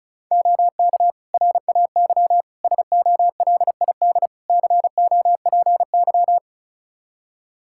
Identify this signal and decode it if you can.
Morse code, heard as a keyed tone, about 36 wpm, 700 Hz. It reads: OK RAY SOLID COPY
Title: Practice QSO - Exchange 3 - More Info - 35wpm